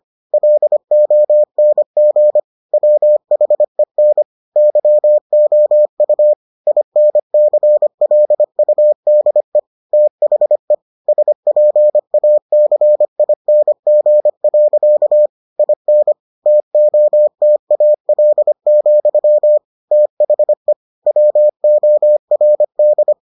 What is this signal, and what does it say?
Morse code, heard as a keyed tone, about 25 wpm, 600 Hz. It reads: LONG WHEN YOU INCLUDE THE SPACING. IN TOTAL, THE WORD